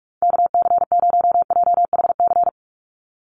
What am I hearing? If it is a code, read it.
Morse code, heard as a keyed tone, about 44 wpm, 690 Hz. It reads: KC0JHC
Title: USA Call Signs 45wpm